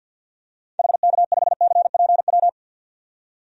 Morse code, heard as a keyed tone, about 53 wpm, 700 Hz. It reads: HK5CPW
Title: DXCC-100 Call Signs - 6 Characters - 50wpm